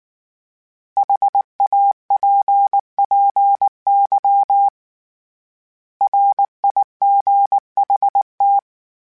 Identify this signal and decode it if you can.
Morse code, heard as a keyed tone, about 19 wpm, 800 Hz. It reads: HAPPY RIGHT
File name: Short Runs / Short Runs-mp3s / Words @18